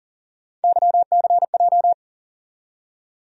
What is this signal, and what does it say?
Morse code, heard as a keyed tone, about 40 wpm, 700 Hz. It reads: YCJ
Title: USA Call Sign Postfixes - 3 Letters 40wpm